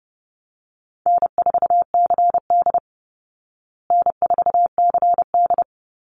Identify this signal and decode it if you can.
Morse code, heard as a keyed tone, about 30 wpm, 700 Hz. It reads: N4CD N4CD
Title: Easy Call Signs Repeated 3x - 30wpm